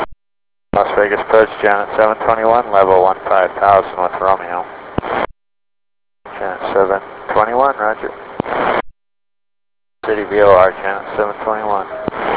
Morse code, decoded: none